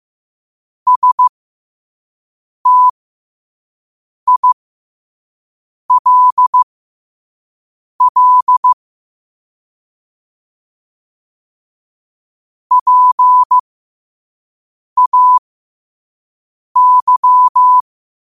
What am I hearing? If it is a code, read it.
Morse code, heard as a keyed tone, about 15 wpm, spaced as for 5 wpm, 1000 Hz. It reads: STILL PAY
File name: qotdfiles / 5wpm